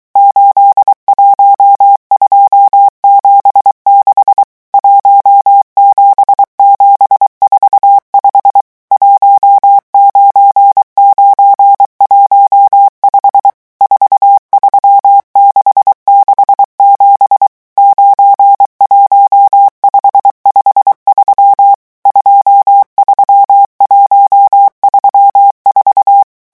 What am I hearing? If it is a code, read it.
Morse code, heard as a keyed tone, about 23 wpm, 800 Hz. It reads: 81276 17745 19915 43667 91553 23134
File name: morse mp3 / revision11 chiffres vitesse 22